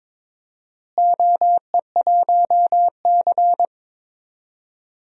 Morse code, heard as a keyed tone, about 22 wpm, 700 Hz. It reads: OE1C
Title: DXCC-100 Call Signs - 4 Characters - Repeated 2x - 22wpm